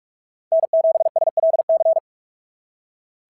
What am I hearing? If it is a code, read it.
Morse code, heard as a keyed tone, about 42 wpm, 690 Hz. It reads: N7SLC
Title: SOTA QSO Practice 45wpm